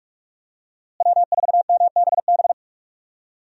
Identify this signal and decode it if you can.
Morse code, heard as a keyed tone, about 46 wpm, 700 Hz. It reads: W4MBB
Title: USA Call Signs - 5 Characters - Repeated 2x - 45wpm